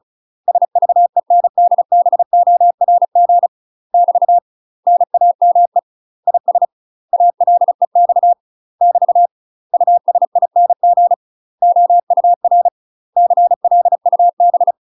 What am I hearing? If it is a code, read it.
Morse code, heard as a keyed tone, about 34 wpm, 700 Hz. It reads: SVENDBORG <BT> NAME IS ALEX <BT> USING OUR CLUB
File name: UK MASTER QSOs TRAINING / 35WPM / UK MASTER QSOs 0034